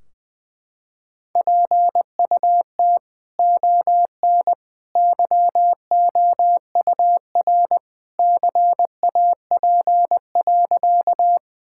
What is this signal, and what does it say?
Morse code, heard as a keyed tone, about 21 wpm, 700 Hz. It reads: PUT ON YOUR CAP.
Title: Sentences from Top 1600 Words 20wpm